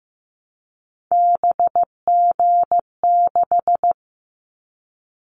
Morse code, heard as a keyed tone, about 15 wpm, 700 Hz. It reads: BG6
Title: Callsign Prefixes for Top 100 DXCC - 15wpm